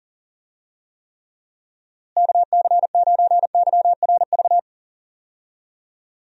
Morse code, heard as a keyed tone, about 37 wpm, 700 Hz. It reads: KC9YRV
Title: USA Call Signs - 6 Characters - Repeated 2x - 40wpm